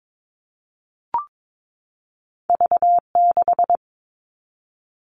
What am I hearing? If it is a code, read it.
Morse code, heard as a keyed tone, about 22 wpm, 700 Hz. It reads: V6
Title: All Cut Numbers - 2-Digit 22wpm